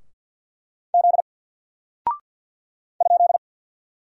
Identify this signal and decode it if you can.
Morse code, heard as a keyed tone, about 49 wpm, 700 Hz. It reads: Z ?
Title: Single Character (Letter:Number:Punctuation:Prosign) 50wpm